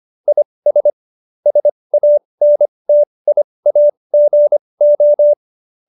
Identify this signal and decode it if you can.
Morse code, heard as a keyed tone, about 25 wpm, spaced as for 20 wpm, 590 Hz. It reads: IS SANTIAGO